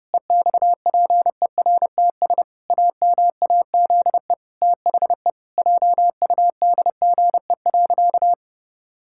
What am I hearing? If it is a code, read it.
Morse code, heard as a keyed tone, about 30 wpm, 700 Hz. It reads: EXPERTS AMAZE THE JUDGE.
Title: Pangram Challenge v2 - 30wpm